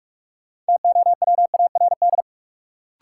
Morse code, heard as a keyed tone, about 43 wpm, 700 Hz. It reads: TOWARD
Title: Sets of 5 Words - New Word Drill - 45wpm